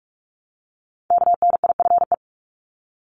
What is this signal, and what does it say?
Morse code, heard as a keyed tone, about 44 wpm, 700 Hz. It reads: KNIFE
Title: Top 1751-2000 Words 45wpm